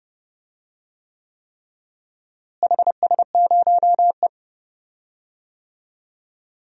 Morse code, heard as a keyed tone, about 29 wpm, 700 Hz. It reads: HS0E
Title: DXCC-100 Call Signs - 4 Characters - Repeated 3x - 30wpm